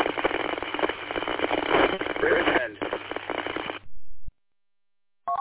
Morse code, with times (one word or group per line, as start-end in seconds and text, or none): none